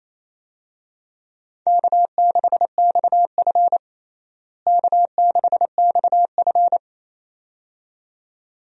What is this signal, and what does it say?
Morse code, heard as a keyed tone, about 28 wpm, 700 Hz. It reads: K6XF K6XF
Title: USA Call Signs - 4 Characters - Repeated 2x - 28wpm